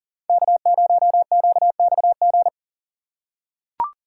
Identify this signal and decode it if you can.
Morse code, heard as a keyed tone, about 40 wpm, 700 Hz. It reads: K0QXG
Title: USA Call Signs - 5 Characters - 4x Character Spacing - 40wpm